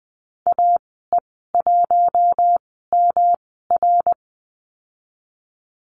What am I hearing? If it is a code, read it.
Morse code, heard as a keyed tone, about 20 wpm, spaced as for 15 wpm, 700 Hz. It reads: AE1MR
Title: 122 Lesson 20g - Number 3 - Call Signs 20wpm